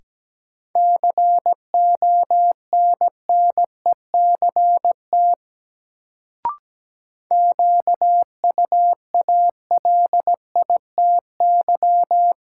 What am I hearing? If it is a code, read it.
Morse code, heard as a keyed tone, about 17 wpm, 700 Hz. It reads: CONNECT QUALITY
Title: Sentences from Top 1500 Words - New Word Drill Alt - 17wpm